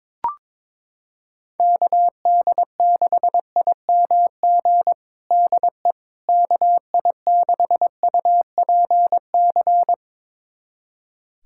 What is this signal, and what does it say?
Morse code, heard as a keyed tone, about 22 wpm, 700 Hz. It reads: KD6IMG DE KI6UPC
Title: USA Callsign DE Callsign 22wpm